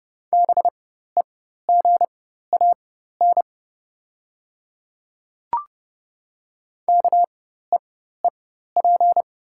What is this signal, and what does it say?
Morse code, heard as a keyed tone, about 30 wpm, spaced as for 14 wpm, 690 Hz. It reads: BEGAN KEEP
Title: 4x Character Spacing - Top 300 Words - 30wpm